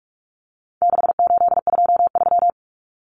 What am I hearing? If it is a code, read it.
Morse code, heard as a keyed tone, about 44 wpm, 700 Hz. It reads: 6823
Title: Numbers - 4-Digit 45wpm